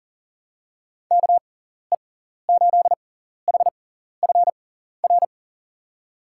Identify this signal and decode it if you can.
Morse code, heard as a keyed tone, about 40 wpm, spaced as for 14 wpm, 700 Hz. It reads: KE8HFR
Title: USA Call Signs - 6 Characters - 6x Character Spacing - 40wpm